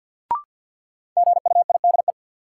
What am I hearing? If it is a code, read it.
Morse code, heard as a keyed tone, about 46 wpm, 700 Hz. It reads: GUIDE
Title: Sentences from Top 600 Words - New Word Drill - 50wpm